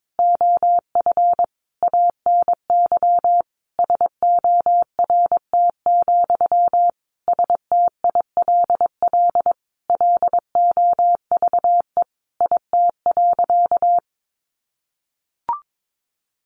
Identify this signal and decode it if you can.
Morse code, heard as a keyed tone, about 22 wpm, 700 Hz. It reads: OF ANY SORT, STILL LOVE IT.
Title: Sentences from Top 1500 Words 22wpm